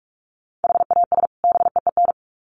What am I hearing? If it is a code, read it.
Morse code, heard as a keyed tone, about 44 wpm, 700 Hz. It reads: HAS BEEN
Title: Sets of 2 Words 45wpm